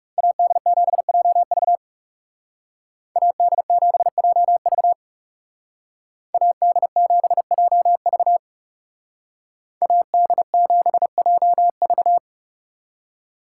Morse code, heard as a keyed tone, about 36 wpm, 700 Hz. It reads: AD7JV AD7JV AD7JV AD7JV
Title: Speed Racing - USA 2x2 Call Signs 30wpm